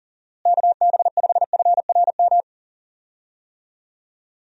Morse code, heard as a keyed tone, about 40 wpm, 700 Hz. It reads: KB5FRM
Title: USA Call Signs - 6 Characters - Repeated 2x - 40wpm